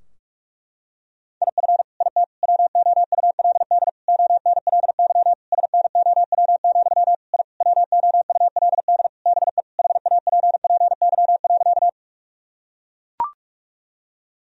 Morse code, heard as a keyed tone, about 43 wpm, 700 Hz. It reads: IF IT WOULD ONLY SNOW, I WOULD BE HAPPY.
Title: Sentences from Top 1000 Words - Review Part 3 - 45wpm